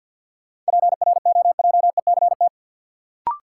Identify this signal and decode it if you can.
Morse code, heard as a keyed tone, about 49 wpm, 700 Hz. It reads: PROJECT
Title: Sentences from Top 1400 Words - New Word Drill Alt - 50wpm